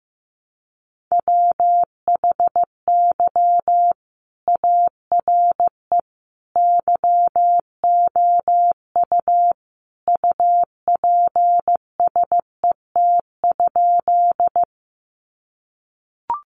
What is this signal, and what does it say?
Morse code, heard as a keyed tone, about 15 wpm, 700 Hz. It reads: WHY ARE YOU UPSET?
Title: Sentences from Top 800 Words 15wpm